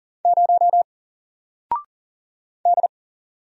Morse code, heard as a keyed tone, about 40 wpm, 700 Hz. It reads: 0 D
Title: Single Letter-Number Alt 40wpm